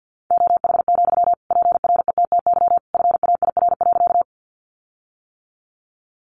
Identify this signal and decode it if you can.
Morse code, heard as a keyed tone, about 50 wpm, 700 Hz. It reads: OH, PRETTY FAIR.
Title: Sentences from Top 1300 Words 50wpm